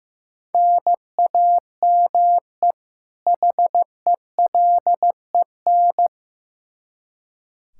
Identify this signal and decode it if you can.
Morse code, heard as a keyed tone, about 15 wpm, 700 Hz. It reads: NAME HELEN
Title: Practice QSO - Exchange 1 - Beginning - 15wpm